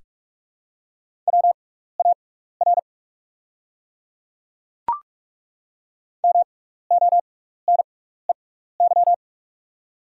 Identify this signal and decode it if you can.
Morse code, heard as a keyed tone, about 43 wpm, spaced as for 15 wpm, 690 Hz. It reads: WAR MONEY
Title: 6x Character Spacing - Top 500 Words - 45wpm